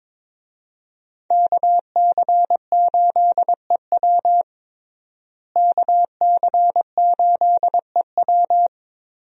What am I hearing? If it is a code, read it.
Morse code, heard as a keyed tone, about 22 wpm, 690 Hz. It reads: KC8EW KC8EW
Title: USA Call Signs - 5 Characters - Repeated 3x - 22wpm